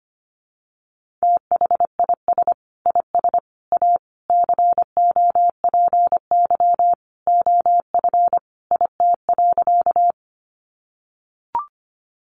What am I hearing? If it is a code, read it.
Morse code, heard as a keyed tone, about 25 wpm, 700 Hz. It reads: THIS IS A COPY OF IT.
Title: Sentences from Top 1400 Words 25wpm